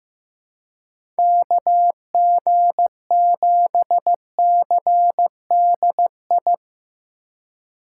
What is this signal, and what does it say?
Morse code, heard as a keyed tone, about 15 wpm, 700 Hz. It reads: KG7CDI
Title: USA Call Signs - 6 Characters - 3x Character Spacing - 15wpm